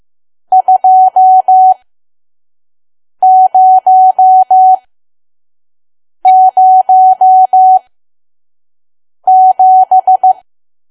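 Morse code, heard as a keyed tone, about 15 wpm, spaced as for 5 wpm, 750 Hz. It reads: 2007
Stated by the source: Morse code practice at 5 WPM from KQ2P.COM.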